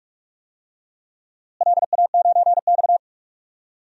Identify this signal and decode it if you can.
Morse code, heard as a keyed tone, about 46 wpm, 700 Hz. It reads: LA9X
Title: DXCC-100 Call Signs - 4 Characters - Repeated 2x - 45wpm